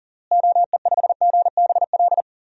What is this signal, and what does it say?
Morse code, heard as a keyed tone, about 40 wpm, 700 Hz. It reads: OE5GBL
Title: DXCC-100 Call Signs - 6 Characters - Repeated 2x - 40wpm